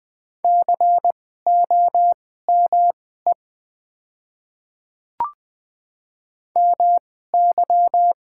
Morse code, heard as a keyed tone, about 20 wpm, spaced as for 14 wpm, 700 Hz. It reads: COME MY
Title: 2x Character Spacing - Top 300 Words - 20wpm